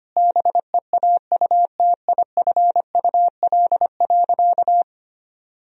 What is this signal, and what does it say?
Morse code, heard as a keyed tone, about 25 wpm, 700 Hz. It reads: BEAUTIFUL.
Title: Sentences from Top 1000 Words - Review Part 2 - 25wpm